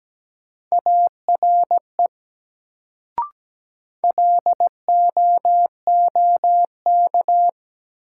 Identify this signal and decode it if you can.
Morse code, heard as a keyed tone, about 17 wpm, 700 Hz. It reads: ARE LOOK
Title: Sentences from Top 100 Words - New Word Drill - 17wpm